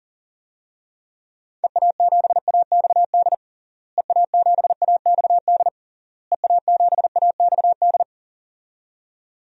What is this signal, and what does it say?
Morse code, heard as a keyed tone, about 40 wpm, 700 Hz. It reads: EA7AXD EA7AXD EA7AXD
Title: DXCC-100 Call Signs - 6 Characters - Repeated 3x - 40wpm